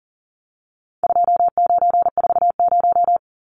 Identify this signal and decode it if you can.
Morse code, heard as a keyed tone, about 40 wpm, 700 Hz. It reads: 2940
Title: Numbers - 4-Digit 40wpm